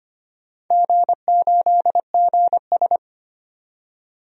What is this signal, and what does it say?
Morse code, heard as a keyed tone, about 25 wpm, 700 Hz. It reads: G8GS
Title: DXCC-100 Call Signs - 4 Characters - 25wpm